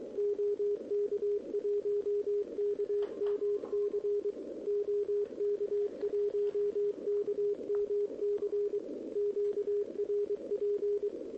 Morse code, heard as a keyed tone, about 22 wpm, 410 Hz, with radio noise. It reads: OK1KMC OK1KMC QRP
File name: OK QRP 2026 / OKQRP2026